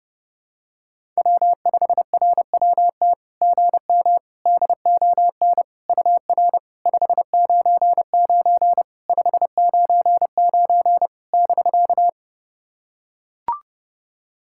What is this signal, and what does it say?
Morse code, heard as a keyed tone, about 31 wpm, 690 Hz. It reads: W5RWT GM DON UR 599 599 <BK>
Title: SOTA QSO Practice 30wpm